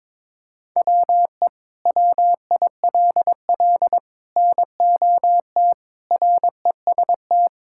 Morse code, heard as a keyed tone, about 22 wpm, 700 Hz. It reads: WE WILL NOT REST
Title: Sentences from Top 1600 Words 22wpm